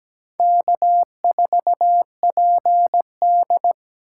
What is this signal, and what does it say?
Morse code, heard as a keyed tone, about 17 wpm, 700 Hz. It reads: K4PD
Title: USA Call Signs - 4 Characters - 2x Character Spacing - 17wpm